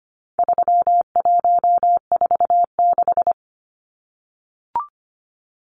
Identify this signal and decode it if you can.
Morse code, heard as a keyed tone, about 25 wpm, 700 Hz. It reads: 3146
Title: Numbers - 4-Digit 25wpm